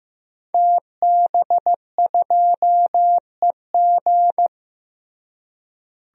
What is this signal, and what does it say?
Morse code, heard as a keyed tone, about 16 wpm, 700 Hz. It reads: TB2EG
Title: DXCC-100 Call Signs - 5 Characters - Repeated 3x - 15wpm